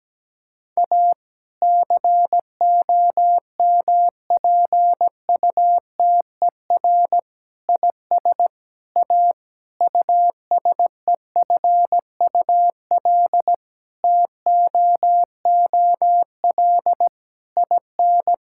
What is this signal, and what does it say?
Morse code, heard as a keyed tone, about 17 wpm, 700 Hz. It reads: A COMPUTER IS A USEFUL TOOL IN
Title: Sentences from Top 1000 Words - Review Part 6 - 17wpm